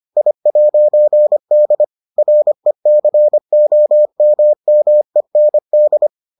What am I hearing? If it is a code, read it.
Morse code, heard as a keyed tone, about 25 wpm, 600 Hz. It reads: I'D RECOMMEND